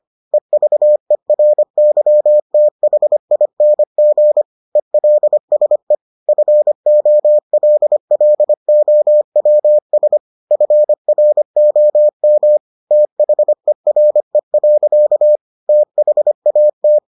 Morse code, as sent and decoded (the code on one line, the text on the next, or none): . ...- . .-. -.-- - .... .. -. --. / . .-.. ... / ..-. --- .-.. .-.. --- .-- ... / ..-. .-. --- -- / - .... .-. .-.-.- / - .... .- -
EVERYTHING ELSE FOLLOWS FROM THERE. THAT